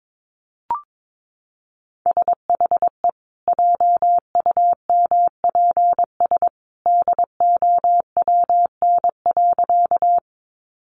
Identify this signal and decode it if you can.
Morse code, heard as a keyed tone, about 22 wpm, 700 Hz. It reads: SHE JUMPS DOWN.
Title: Easy Phrases 22wpm